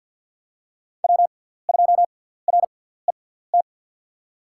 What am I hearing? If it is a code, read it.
Morse code, heard as a keyed tone, about 47 wpm, spaced as for 17 wpm, 700 Hz. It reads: W2RET
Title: USA Call Signs - 5 Characters - 6x Character Spacing - 50wpm